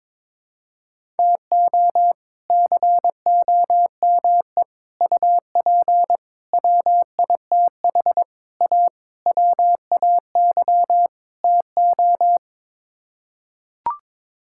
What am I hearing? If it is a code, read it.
Morse code, heard as a keyed tone, about 22 wpm, 700 Hz. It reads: TO COME UP WITH A WAY TO
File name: practice / 22wpm / Sets of 7 Words - 22wpm